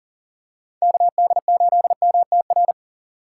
Morse code, heard as a keyed tone, about 40 wpm, 700 Hz. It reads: KD8MTR
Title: USA Call Signs - 6 Characters - 2x Character Spacing - 40wpm